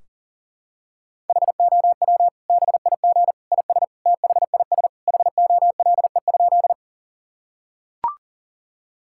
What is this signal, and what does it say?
Morse code, heard as a keyed tone, about 40 wpm, 700 Hz. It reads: HOW BIG IS THIS HOLE?
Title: Sentences from Top 1600 Words 40wpm